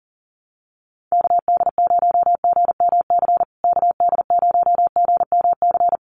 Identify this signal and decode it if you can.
Morse code, heard as a keyed tone, about 40 wpm, 700 Hz. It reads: KD0GMC KD0GMC
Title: USA Call Signs Repeated 3x 40wpm